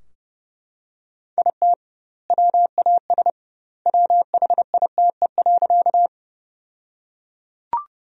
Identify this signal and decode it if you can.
Morse code, heard as a keyed tone, about 29 wpm, 700 Hz. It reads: IT WAS WHITE.
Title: 2x Word-Spacing - Sentences from Top 300 Words - 30wpm